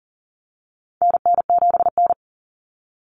Morse code, heard as a keyed tone, about 40 wpm, 700 Hz. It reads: NN7N
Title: Common Cut Numbers - 4-Digit 40wpm